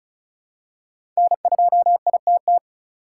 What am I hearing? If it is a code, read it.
Morse code, heard as a keyed tone, about 36 wpm, 700 Hz. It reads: N2ITT
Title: USA Call Signs - 5 Characters - 3x Character Spacing - 35wpm